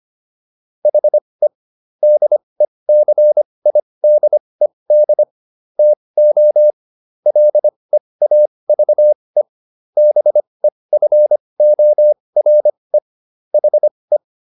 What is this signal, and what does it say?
Morse code, heard as a keyed tone, about 25 wpm, spaced as for 20 wpm, 600 Hz. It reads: HE DECIDED TO LEAVE BEFORE HE